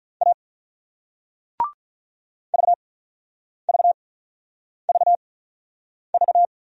Morse code, heard as a keyed tone, about 46 wpm, spaced as for 9 wpm, 700 Hz. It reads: A VVVV